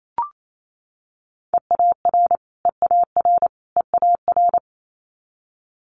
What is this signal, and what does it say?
Morse code, heard as a keyed tone, about 28 wpm, 700 Hz. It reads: EAR EAR EAR
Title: Multi-letter Phonemes Repeated 3x 28wpm